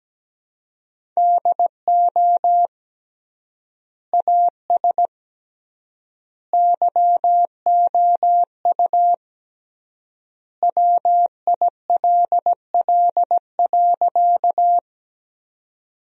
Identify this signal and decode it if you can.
Morse code, heard as a keyed tone, about 17 wpm, 700 Hz. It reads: DO AS YOU WILL.
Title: 3x Word-Spacing - Sentences from Top 100 Words - 17wpm